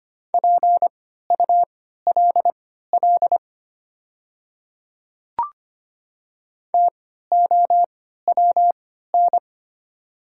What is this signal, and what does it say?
Morse code, heard as a keyed tone, about 25 wpm, spaced as for 14 wpm, 700 Hz. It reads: PULL TOWN
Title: 3x Character Spacing - Top 500 Words - 25wpm